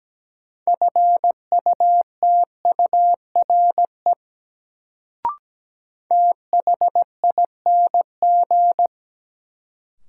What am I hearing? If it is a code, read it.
Morse code, heard as a keyed tone, about 17 wpm, 700 Hz. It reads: FUTURE THING
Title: Sets of 3 Words - New Word Drill - 17wpm